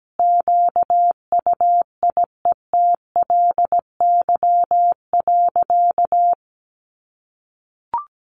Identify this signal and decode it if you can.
Morse code, heard as a keyed tone, about 17 wpm, 700 Hz. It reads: QUIETLY.
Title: A Sound Of Thunder 17wpm